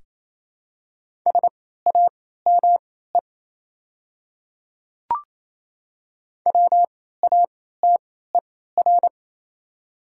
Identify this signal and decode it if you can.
Morse code, heard as a keyed tone, about 28 wpm, spaced as for 16 wpm, 700 Hz. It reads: SAME WATER
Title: 3x Character Spacing - Top 300 Words - 28wpm